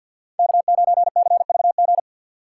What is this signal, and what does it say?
Morse code, heard as a keyed tone, about 66 wpm, spaced as for 58 wpm, 700 Hz. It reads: K9CVG
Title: Speed Racing - USA Call Signs 50wpm